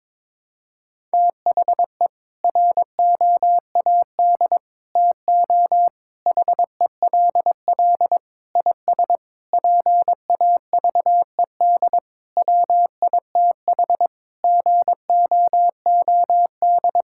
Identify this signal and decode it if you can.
Morse code, heard as a keyed tone, about 22 wpm, 700 Hz. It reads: THE ROAD TO HELL IS PAVED WITH GOOD
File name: practice / 22wpm / Sentences with 8-Word Phrases - 22wpm